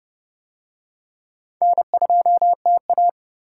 Morse code, heard as a keyed tone, about 29 wpm, 700 Hz. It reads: N2TA